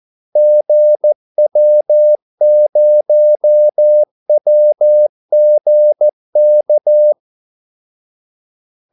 Morse code, heard as a keyed tone, about 14 wpm, 600 Hz. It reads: GW0WGK
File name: Callsigns-14wpm